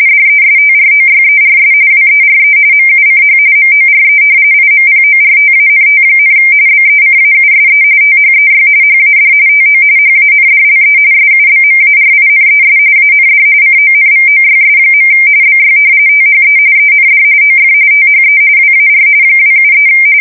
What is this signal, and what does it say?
RTTY, about 45.45 baud. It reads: DESTROYERS WERE MUCH MORE IMPORTANT, BECAUSE THEY HAD TO HANDLE SUBMARINES AND CONVOYS. BY THE END OF THE WAR 273
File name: fsk170 / usnavy4